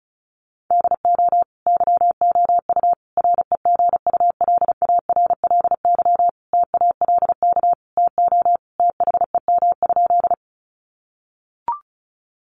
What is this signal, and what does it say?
Morse code, heard as a keyed tone, about 34 wpm, 700 Hz. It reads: DO YOU REGULARLY TALK TO THEM?
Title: Sentences from Top 1400 Words 35wpm